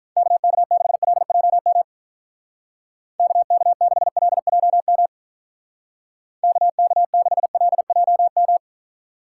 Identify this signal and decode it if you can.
Morse code, heard as a keyed tone, about 46 wpm, 700 Hz. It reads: KK6LJM KK6LJM KK6LJM